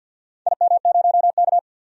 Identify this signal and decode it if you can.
Morse code, heard as a keyed tone, about 51 wpm, 700 Hz. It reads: IM0K
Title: DXCC-100 Call Signs - 4 Characters - 50wpm